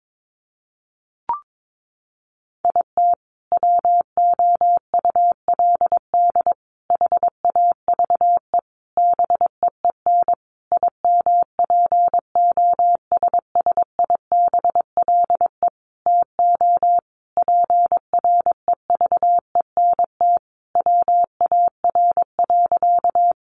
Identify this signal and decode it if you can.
Morse code, heard as a keyed tone, about 22 wpm, 700 Hz. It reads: IT WOULD HAVE BEEN IMPOSSIBLE TO PREVENT WAR.
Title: Sentences from Top 1100 Words 22wpm